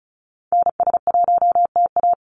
Morse code, heard as a keyed tone, about 34 wpm, 700 Hz. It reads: NS1TA
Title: NA SOTA Activator Callsigns 35wpm